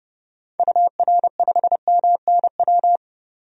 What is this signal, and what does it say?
Morse code, heard as a keyed tone, about 32 wpm, 700 Hz. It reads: UR5MNW
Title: DXCC-100 Call Signs - 6 Characters - 30wpm